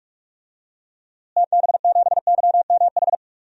Morse code, heard as a keyed tone, about 46 wpm, 700 Hz. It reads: TB7YMH
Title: DXCC-100 Call Signs - 6 Characters - Repeated 3x - 45wpm